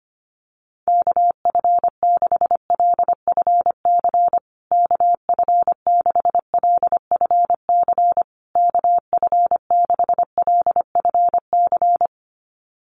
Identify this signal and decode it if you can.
Morse code, heard as a keyed tone, about 25 wpm, 700 Hz. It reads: KF6LFC KF6LFC KF6LFC
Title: USA Call Signs Repeated 3x 25wpm